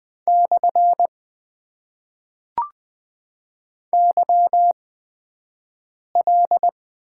Morse code, heard as a keyed tone, about 20 wpm, spaced as for 5 wpm, 700 Hz. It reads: / YL
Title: Top 90 QSO Elements - 8x Character Spacing - 20wpm